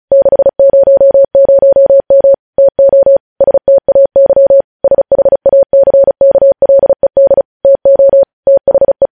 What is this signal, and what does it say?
Morse code, heard as a keyed tone, about 36 wpm, 550 Hz. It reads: 600M TO STAY SHACKLED TO THE
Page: News headlines in Morse code at 35 wpm